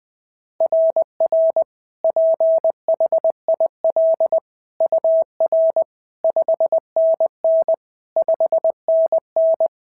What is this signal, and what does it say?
Morse code, heard as a keyed tone, about 21 wpm, 690 Hz. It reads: RR PHIL UR 5NN 5NN
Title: SOTA QSO Practice 20wpm